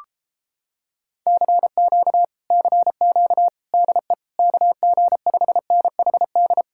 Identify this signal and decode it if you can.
Morse code, heard as a keyed tone, about 29 wpm, 700 Hz. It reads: CQ CQ DE KG5NHD